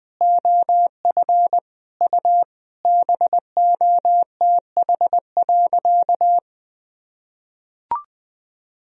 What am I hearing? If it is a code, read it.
Morse code, heard as a keyed tone, about 20 wpm, 700 Hz. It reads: OF U BOTH.
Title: Sentences from Top 300 Words and Abbreviations 20wpm